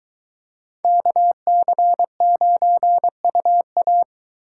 Morse code, heard as a keyed tone, about 23 wpm, 700 Hz. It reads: KC9UA